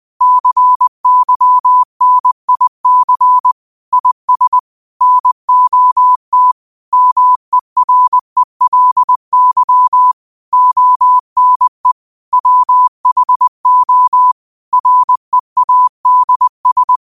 Morse code, heard as a keyed tone, about 20 wpm, 1000 Hz. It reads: CYNIC IS NOT MERELY ONE WHO READS
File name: qotdfiles / 20wpm